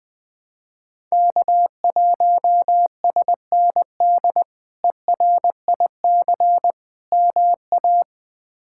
Morse code, heard as a keyed tone, about 20 wpm, 700 Hz. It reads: K1SND ERIC MA
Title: K1USN SST QSO Practice - 20wpm